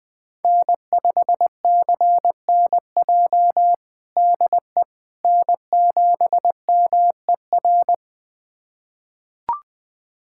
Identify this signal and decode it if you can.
Morse code, heard as a keyed tone, about 20 wpm, 700 Hz. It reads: N5CNJ DE N7MER
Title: USA Callsign DE Callsign - 5 Characters - 20wpm